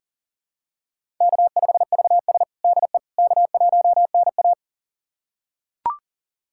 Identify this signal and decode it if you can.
Morse code, heard as a keyed tone, about 40 wpm, 700 Hz. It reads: K5VS DE K1NA
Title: USA Callsign DE Callsign - 4 Characters - 40wpm